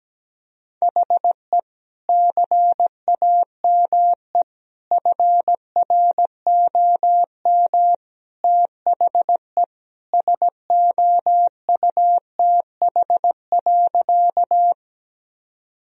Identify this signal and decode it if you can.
Morse code, heard as a keyed tone, about 17 wpm, 700 Hz. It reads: HE CAME FROM THE SOUTH.
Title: Sentences from Top 1000 Words - Review Part 2 - 17wpm